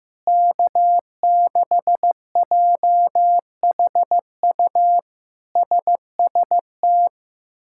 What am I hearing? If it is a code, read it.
Morse code, heard as a keyed tone, about 15 wpm, 700 Hz. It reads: K6JHU SST
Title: K1USN SST QSO Practice - 15wpm